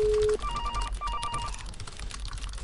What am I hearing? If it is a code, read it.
There is no code.